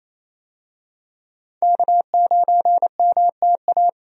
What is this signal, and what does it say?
Morse code, heard as a keyed tone, about 28 wpm, 700 Hz. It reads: K9MTA
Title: USA Call Signs - 5 Characters - Repeated 2x - 28wpm